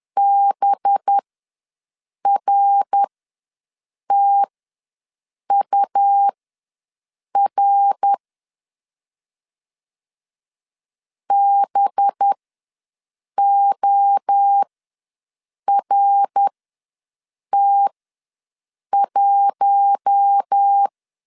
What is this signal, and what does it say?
Morse code, heard as a keyed tone, about 10 wpm, spaced as for 6 wpm, 800 Hz. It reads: BRTUR BORT1
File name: LEZIONI / 5 LBVC / 5 LB ws 10wpm 3616